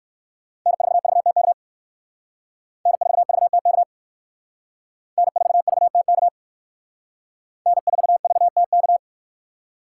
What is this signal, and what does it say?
Morse code, heard as a keyed tone, about 59 wpm, 700 Hz. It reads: N4VTK N4VTK N4VTK N4VTK